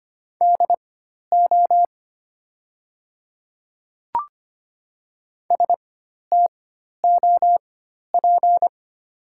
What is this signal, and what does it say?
Morse code, heard as a keyed tone, about 25 wpm, spaced as for 12 wpm, 700 Hz. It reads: DO STOP
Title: 4x Character Spacing - Top 300 Words - 25wpm